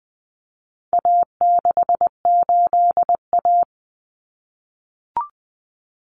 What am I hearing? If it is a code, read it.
Morse code, heard as a keyed tone, about 20 wpm, 700 Hz. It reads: A68A
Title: Common Cut Numbers - 4-Digit 20wpm